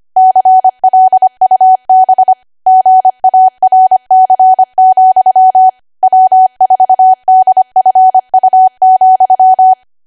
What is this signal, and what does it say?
Morse code, heard as a keyed tone, about 25 wpm, 750 Hz. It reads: CLUB GARC, W4DFU,
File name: Archive / 25 WPM / 25WPM